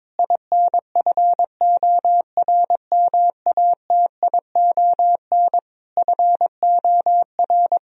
Morse code, heard as a keyed tone, about 22 wpm, 700 Hz. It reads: INFORMATION FOR
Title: Sentences from Top 1000 Words - Review Part 3 - 22wpm